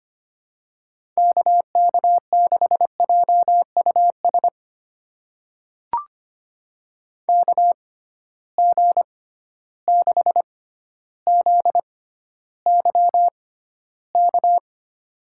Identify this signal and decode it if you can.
Morse code, heard as a keyed tone, about 25 wpm, spaced as for 13 wpm, 700 Hz. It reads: KK6JUS KG6ZYK
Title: USA Call Signs - 6 Characters - 6x Character Spacing - 25wpm